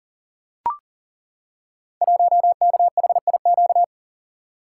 Morse code, heard as a keyed tone, about 39 wpm, 700 Hz. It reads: 1KHIQ
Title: Code Groups - 5 Letter:Numbers - 40wpm